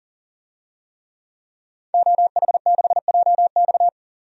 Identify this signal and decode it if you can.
Morse code, heard as a keyed tone, about 37 wpm, 700 Hz. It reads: OH6JX
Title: DXCC-100 Call Signs - 5 Characters - 40wpm